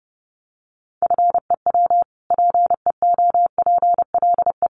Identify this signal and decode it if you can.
Morse code, heard as a keyed tone, about 30 wpm, 700 Hz. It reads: FEW PEOPLE
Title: Sentences from Top 600 Words 30wpm